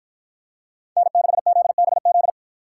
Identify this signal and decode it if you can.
Morse code, heard as a keyed tone, about 53 wpm, 700 Hz. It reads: N6ZBZ